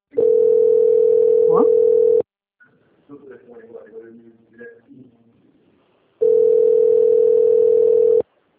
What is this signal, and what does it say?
Ringback tone (North America), 440 + 480 Hz, with speech.
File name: out- - - - - .19152